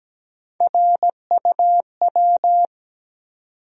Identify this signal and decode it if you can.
Morse code, heard as a keyed tone, about 17 wpm, 700 Hz. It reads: RUW
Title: USA Call Sign Postfixes - 3 Letters 17wpm